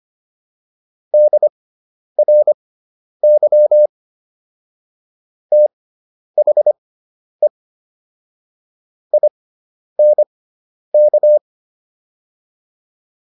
Morse code, heard as a keyed tone, about 25 wpm, spaced as for 10 wpm, 600 Hz. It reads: DRY THE INK